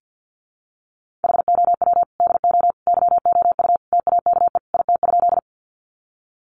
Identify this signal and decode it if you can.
Morse code, heard as a keyed tone, about 48 wpm, 700 Hz. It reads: HOW DO YOU TAKE IT?
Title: Sentences from Top 200 Words 50wpm